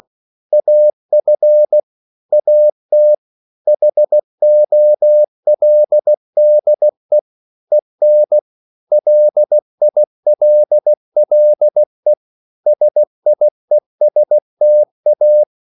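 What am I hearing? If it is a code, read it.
Morse code, heard as a keyed tone, about 16 wpm, 600 Hz. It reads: AF AT HOLDE EN LILLE SIESTA
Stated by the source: Slow (QRS) Morse code at 16 wpm, from CW med Gnister.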